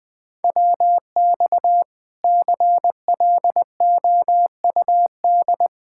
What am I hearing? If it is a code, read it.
Morse code, heard as a keyed tone, about 21 wpm, 700 Hz. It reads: WX CLOUD
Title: Speed-Racing Common QSO Phrases - 20wpm